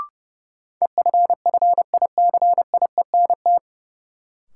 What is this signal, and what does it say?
Morse code, heard as a keyed tone, about 30 wpm, 700 Hz. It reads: EFFICIENT
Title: Sentences from Top 1300 Words - New Word Drill - 30wpm